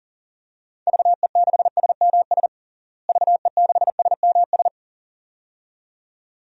Morse code, heard as a keyed tone, about 40 wpm, 700 Hz. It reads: VE6SMS VE6SMS
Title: DXCC-100 Call Signs - 6 Characters - Repeated 2x - 40wpm